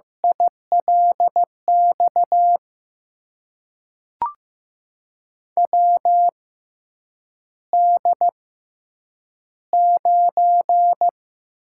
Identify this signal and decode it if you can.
Morse code, heard as a keyed tone, about 15 wpm, spaced as for 8 wpm, 700 Hz. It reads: ILX WD9
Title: USA Call Signs - 6 Characters - 6x Character Spacing - 15wpm